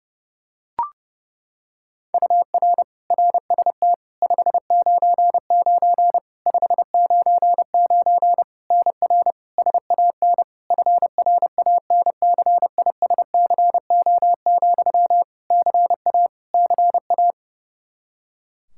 Morse code, heard as a keyed tone, about 32 wpm, 700 Hz. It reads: UR RST 599 599 NR SAN FRANCISCO, CA CA
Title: Practice QSO - Exchange 1 - Beginning - 30wpm